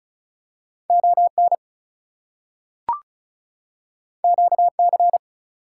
Morse code, heard as a keyed tone, about 36 wpm, 700 Hz. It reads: ON QC
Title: US States and Canadian Provinces 35wpm